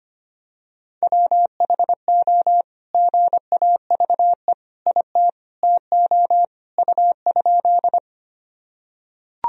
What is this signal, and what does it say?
Morse code, heard as a keyed tone, about 25 wpm, 700 Hz. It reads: WHO GAVE IT TO U?
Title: Sentences from Top 500 Words and Abbreviations 25wpm